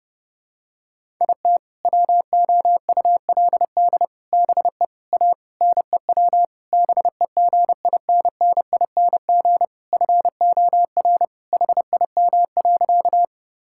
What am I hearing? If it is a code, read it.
Morse code, heard as a keyed tone, about 28 wpm, 700 Hz. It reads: IT WOULD BE A NEW BEGINNING FOR HIM.
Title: Sentences from Top 1000 Words - Review Part 5 - 30wpm